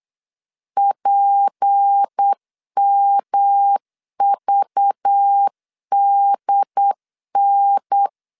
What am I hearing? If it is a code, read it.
Morse code, heard as a keyed tone, about 8 wpm, 800 Hz. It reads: PMVDN
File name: LEZIONI / 6 PKX0 / 6 P 8wpm 3025